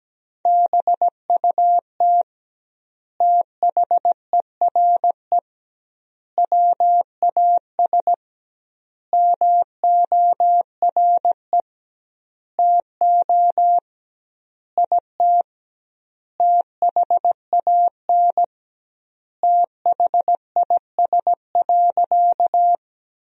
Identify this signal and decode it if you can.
Morse code, heard as a keyed tone, about 17 wpm, 700 Hz. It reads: BUT THERE WAS MORE TO IT THAN THIS.
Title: 2x Word-Spacing - Sentences from Top 100 Words - 17wpm